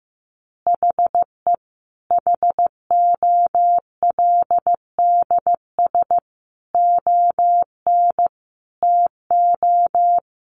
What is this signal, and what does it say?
Morse code, heard as a keyed tone, about 14 wpm, 700 Hz. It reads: HE HOLDS ON TO
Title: Additional Easy Phrases 15wpm